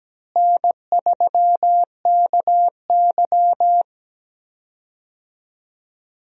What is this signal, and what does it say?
Morse code, heard as a keyed tone, about 17 wpm, 700 Hz. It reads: N3KY
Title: USA Call Signs - 4 Characters - Repeated 2x - 17wpm